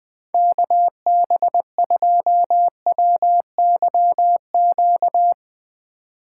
Morse code, heard as a keyed tone, about 21 wpm, 700 Hz. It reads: KB2WYQ
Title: Speed Racing - USA 2x3 Call Signs 17wpm